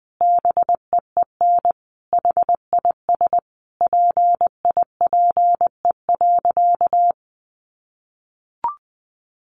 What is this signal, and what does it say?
Morse code, heard as a keyed tone, about 20 wpm, 700 Hz. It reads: BEEN HIS PIPE.
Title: Sentences from Top 1500 Words 20wpm